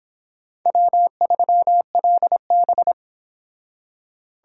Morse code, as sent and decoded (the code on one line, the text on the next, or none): .-- ...-- .-.. -...
W3LB